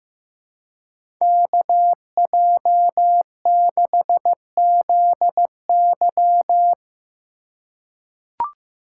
Morse code, heard as a keyed tone, about 15 wpm, 700 Hz. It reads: KJ6ZY
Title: USA Call Signs - 5 Characters - 2x Character Spacing - 15wpm